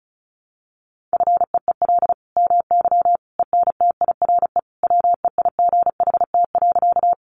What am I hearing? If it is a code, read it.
Morse code, heard as a keyed tone, about 34 wpm, 700 Hz. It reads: FEEL MY ENTIRE WEIGHT.
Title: Sentences from Top 800 Words 35wpm